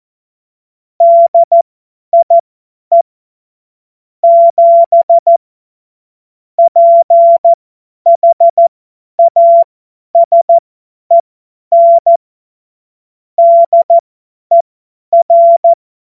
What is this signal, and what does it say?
Morse code, heard as a keyed tone, about 14 wpm, spaced as for 10 wpm, 680 Hz. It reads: DIE 7 PHASEN DER
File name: Morsekurs / Witze / Witz 09. Dezember 2025 50 F70